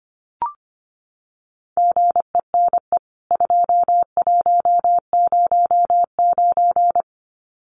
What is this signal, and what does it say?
Morse code, heard as a keyed tone, about 25 wpm, 700 Hz. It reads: GENE 2109
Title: CWT Exchange 25wpm